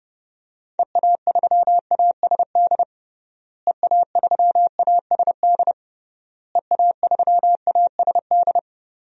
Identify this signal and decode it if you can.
Morse code, heard as a keyed tone, about 30 wpm, 700 Hz. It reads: EA3ASD EA3ASD EA3ASD
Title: DXCC-100 Call Signs - 6 Characters - Repeated 3x - 30wpm